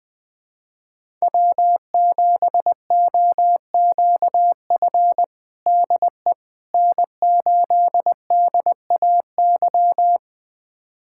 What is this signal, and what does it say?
Morse code, heard as a keyed tone, about 20 wpm, 700 Hz. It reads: W7OQF DE N8DAY
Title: USA Callsign DE Callsign - 5 Characters - 20wpm